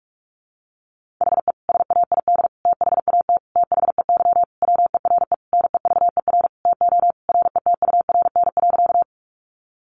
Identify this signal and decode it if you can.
Morse code, heard as a keyed tone, about 44 wpm, 700 Hz. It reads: HE SAID THAT THEY WERE NEVER TO RETURN.
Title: Sentences from Top 800 Words 45wpm